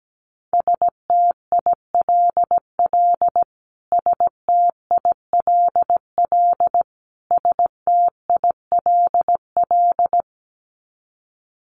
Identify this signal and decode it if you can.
Morse code, heard as a keyed tone, about 17 wpm, 700 Hz. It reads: STILL STILL STILL
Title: Top 500 Words Repeated 17wpm